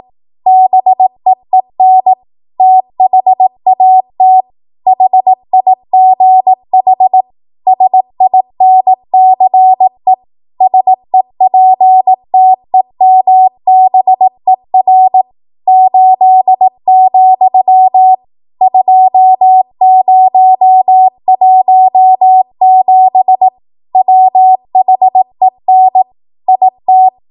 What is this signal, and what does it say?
Morse code, heard as a keyed tone, about 18 wpm, 750 Hz. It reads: BEEN THAT HIGH SINCE SEPTEMBER 8, 2017 WHEN IT
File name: Archive / 18 WPM / 18WPM